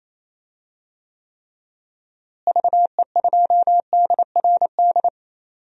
Morse code, heard as a keyed tone, about 28 wpm, 700 Hz. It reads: VE2DRD
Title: DXCC-100 Call Signs - 6 Characters - Repeated 3x - 28wpm